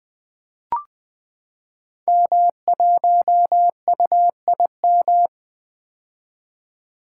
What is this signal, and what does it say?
Morse code, heard as a keyed tone, about 20 wpm, 700 Hz. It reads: M1UIM
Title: Code Groups - 5 Letter:Numbers - 20wpm